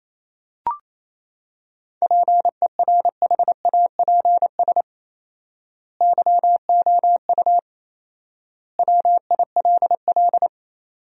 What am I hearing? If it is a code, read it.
Morse code, heard as a keyed tone, about 28 wpm, 700 Hz. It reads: PERHAPS YOU WILL
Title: 4x Word-Spacing - Sentences from Top 500 Words - 28wpm